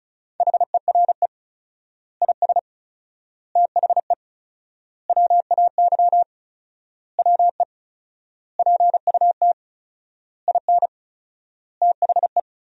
Morse code, heard as a keyed tone, about 34 wpm, 700 Hz. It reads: HERE IS THE WAY WE PUT IN THE
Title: 4x Word-Spacing - Sentences from Top 200 Words - 35wpm